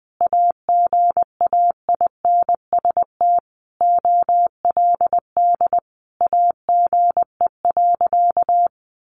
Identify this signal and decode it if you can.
Morse code, heard as a keyed tone, about 20 wpm, 700 Hz. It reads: AGAINST OLD AGE.
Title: Sentences from Top 1100 Words 20wpm